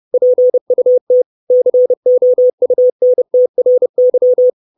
Morse code, heard as a keyed tone, about 30 wpm, 500 Hz. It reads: PUT COUNTRY
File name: 632 Words 30WPM